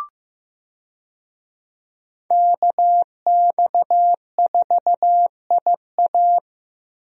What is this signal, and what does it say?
Morse code, heard as a keyed tone, about 15 wpm, 700 Hz. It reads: KX4IA
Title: USA Call Signs - 5 Characters - Repeated 3x - 15wpm